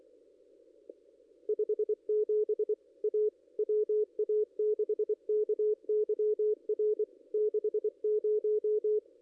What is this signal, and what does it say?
Morse code, heard as a keyed tone, about 24 wpm, 440 Hz. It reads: 57 A WA6KYR 60
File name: contests / ss-cw-k5tr-op-n5rz -2200